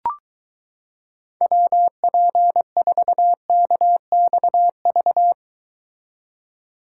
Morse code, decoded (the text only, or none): WP4KXV